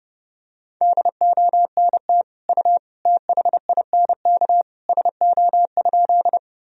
Morse code, heard as a keyed tone, about 30 wpm, 700 Hz. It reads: DONT U THINK SO?
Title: Sentences from Top 300 Words and Abbreviations 30wpm